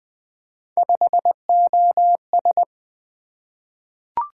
Morse code, heard as a keyed tone, about 20 wpm, 700 Hz. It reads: 5OS
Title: Three Letter-Number Combinations - 20wpm